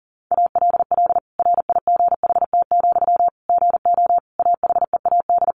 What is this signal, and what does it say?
Morse code, heard as a keyed tone, about 40 wpm, 700 Hz. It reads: ALL RIGHT, GO AHEAD
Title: Sentences from Top 1100 Words 40wpm